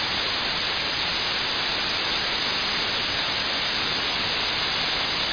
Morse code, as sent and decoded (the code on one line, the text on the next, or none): none